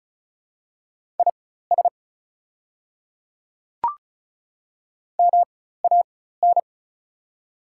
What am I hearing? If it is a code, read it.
Morse code, heard as a keyed tone, about 36 wpm, spaced as for 16 wpm, 700 Hz. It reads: IS MAN
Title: 4x Character Spacing - Top 200 Words - 35wpm